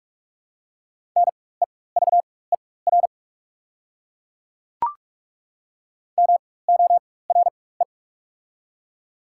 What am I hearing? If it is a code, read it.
Morse code, heard as a keyed tone, about 43 wpm, spaced as for 21 wpm, 700 Hz. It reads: NEVER MORE
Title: 4x Character Spacing - Top 500 Words - 45wpm